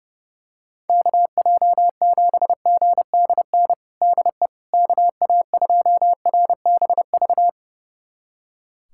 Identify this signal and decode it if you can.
Morse code, heard as a keyed tone, about 29 wpm, 700 Hz. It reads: KJ7GDN DE KA2RBV
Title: USA Callsign DE Callsign - 6 Characters - 30wpm